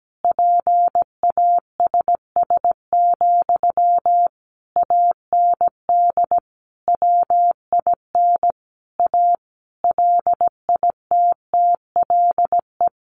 Morse code, heard as a keyed tone, about 17 wpm, 700 Hz. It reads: PASS, AND WIN A LITTLE
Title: Sentences from Top 1500 Words 17wpm